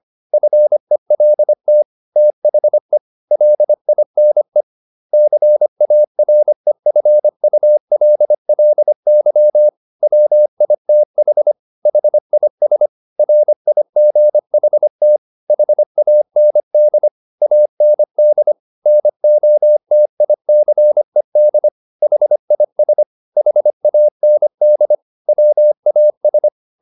Morse code, heard as a keyed tone, about 25 wpm, 600 Hz. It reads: FELT THE LINE CAREFULLY WITH HIS RIGHT HAND AND NOTICED HIS HAND WAS
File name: the old man and the sea part 1 25wpm